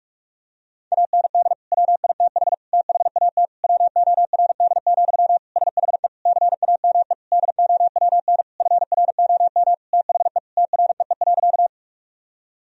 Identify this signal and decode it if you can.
Morse code, heard as a keyed tone, about 44 wpm, 700 Hz. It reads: AND WITH THAT WORD, SHE CAME DOWN FROM THE TREE.
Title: Sentences from Top 1000 Words - Review Part 1 - 45wpm